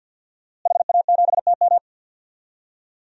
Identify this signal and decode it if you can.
Morse code, heard as a keyed tone, about 52 wpm, 700 Hz. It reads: HA7TM
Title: DXCC-100 Call Signs - 5 Characters - Repeated 2x - 50wpm